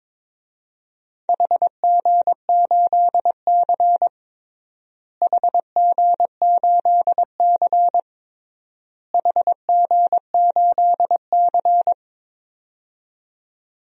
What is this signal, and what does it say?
Morse code, heard as a keyed tone, about 21 wpm, 700 Hz. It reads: HG8C HG8C HG8C
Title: DXCC-100 Call Signs - 4 Characters - Repeated 3x - 22wpm